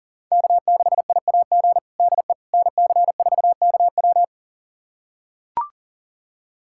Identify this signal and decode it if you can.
Morse code, heard as a keyed tone, about 37 wpm, 700 Hz. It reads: K6IAG DE NC4KW
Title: USA Callsign DE Callsign - 5 Characters - 40wpm